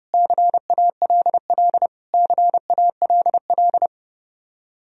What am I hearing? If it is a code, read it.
Morse code, heard as a keyed tone, about 30 wpm, 700 Hz. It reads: CALL CALL
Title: Top 100 Words Repeated 30wpm